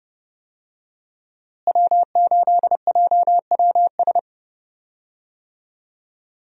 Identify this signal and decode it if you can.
Morse code, heard as a keyed tone, about 30 wpm, 700 Hz. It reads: W8JWS
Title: USA Call Signs - 5 Characters - Repeated 2x - 30wpm